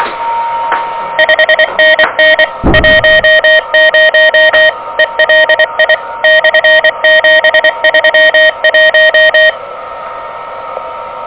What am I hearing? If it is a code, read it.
Morse code, heard as a keyed tone, about 24 wpm, 1850 Hz, with noise. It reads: 5NN 10 ELI/731
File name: OK QRP 2023 / OK QRP 2023